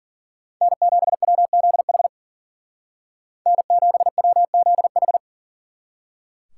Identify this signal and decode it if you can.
Morse code, heard as a keyed tone, about 48 wpm, 700 Hz. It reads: N7WZH N7WZH
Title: Speed Racing - USA 1x3 Call Signs 40wpm